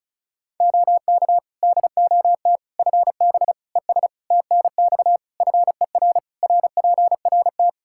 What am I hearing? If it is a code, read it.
Morse code, heard as a keyed tone, about 34 wpm, 700 Hz. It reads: OK DOT FB ES TNX FER RPRT
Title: Practice QSO - Exchange 2 - Info - 35wpm